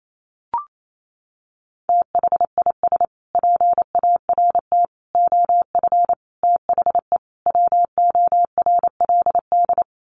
Morse code, heard as a keyed tone, about 28 wpm, 700 Hz. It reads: THIS PART OF THE WORLD
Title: Sets of 5 Words Encore2 28wpm